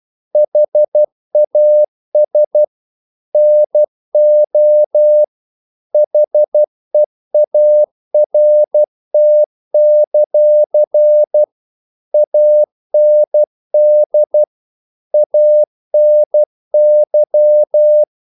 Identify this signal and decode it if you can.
Morse code, heard as a keyed tone, about 12 wpm, 600 Hz. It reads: HAS NO HEART; AND ANY
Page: English quotations in CW (telegraphy - morse)